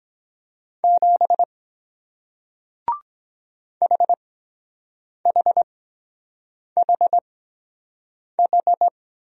Morse code, heard as a keyed tone, about 23 wpm, spaced as for 7 wpm, 700 Hz. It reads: 7 HHHH